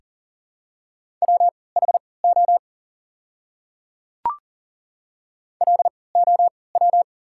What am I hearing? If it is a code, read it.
Morse code, heard as a keyed tone, about 40 wpm, spaced as for 23 wpm, 690 Hz. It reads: WHO LOW
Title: 3x Character Spacing - Top 300 Words - 40wpm